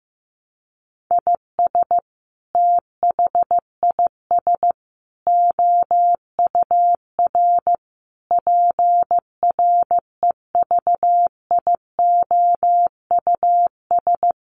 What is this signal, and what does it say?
Morse code, heard as a keyed tone, about 15 wpm, 700 Hz. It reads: IS THIS OUR PREVIOUS
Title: Sentences from Top 1500 Words 15wpm